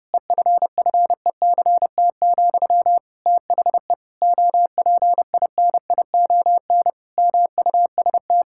Morse code, heard as a keyed tone, about 30 wpm, 700 Hz. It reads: EFFECT, THE OPINION MUST
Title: Sentences from Top 1600 Words 30wpm